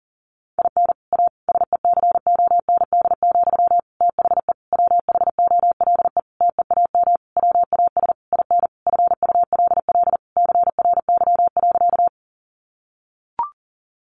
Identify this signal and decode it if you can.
Morse code, heard as a keyed tone, about 40 wpm, 700 Hz. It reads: IN A SECOND, THE WHOLE TEAM WAS IN FULL CRY.
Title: Sentences from Top 1500 Words 40wpm